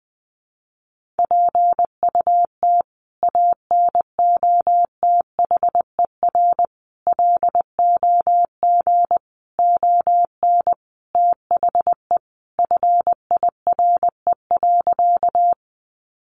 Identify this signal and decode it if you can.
Morse code, heard as a keyed tone, about 20 wpm, 700 Hz. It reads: PUT ANOTHER LOG ON THE FIRE.
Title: Sentences from Top 1400 Words 20wpm